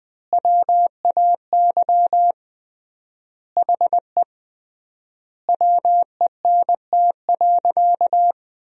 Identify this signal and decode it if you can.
Morse code, heard as a keyed tone, about 20 wpm, 700 Hz. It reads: WAY HE WENT.
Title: 3x Word-Spacing - Sentences from Top 200 Words - 20wpm